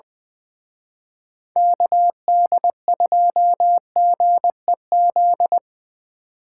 Morse code, heard as a keyed tone, about 24 wpm, 700 Hz. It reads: KD2GEZ
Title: Speed Racing - USA 2x3 Call Signs 20wpm